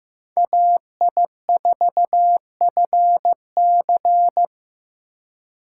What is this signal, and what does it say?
Morse code, heard as a keyed tone, about 18 wpm, 700 Hz. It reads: AI4FC
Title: Speed Racing - USA 2x2 Call Signs 15wpm